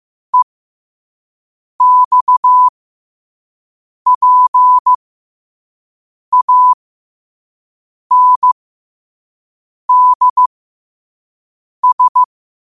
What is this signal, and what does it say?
Morse code, heard as a keyed tone, about 15 wpm, spaced as for 5 wpm, 1000 Hz. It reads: EXPANDS
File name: qotdfiles / 5wpm